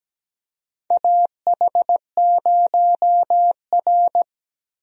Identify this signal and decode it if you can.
Morse code, heard as a keyed tone, about 17 wpm, 700 Hz. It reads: AH0R
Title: USA Call Signs - 4 Characters - 17wpm